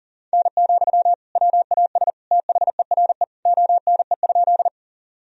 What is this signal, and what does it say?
Morse code, heard as a keyed tone, about 37 wpm, 700 Hz. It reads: N, WAS THERE ONE?
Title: Sentences from Top 100 Words and Abbreviations 40wpm